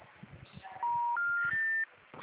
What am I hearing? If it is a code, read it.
Special information tone.